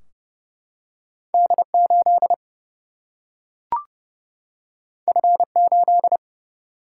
Morse code, heard as a keyed tone, about 30 wpm, 700 Hz. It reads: D8 F8
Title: Single-Letter Single-Number 30wpm